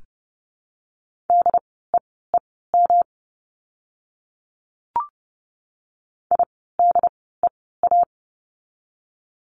Morse code, heard as a keyed tone, about 30 wpm, spaced as for 17 wpm, 700 Hz. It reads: DEEM IDEA
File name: course / 30:10wpm / 108 Lesson 18f - Letter m - Words 30wpm